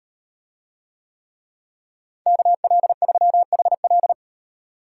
Morse code, heard as a keyed tone, about 35 wpm, 700 Hz. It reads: KL3HL